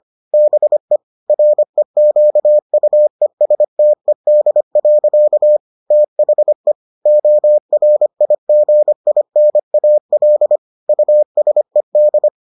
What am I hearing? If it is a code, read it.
Morse code, heard as a keyed tone, about 25 wpm, 600 Hz. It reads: BE REQUESTED. THE ORIGINAL USED